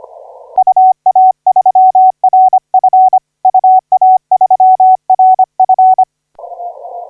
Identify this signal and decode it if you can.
Morse code, heard as a keyed tone, about 25 wpm, 750 Hz. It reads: UA3RF UA3RF
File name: 5H1DX 3 1141UTC 15m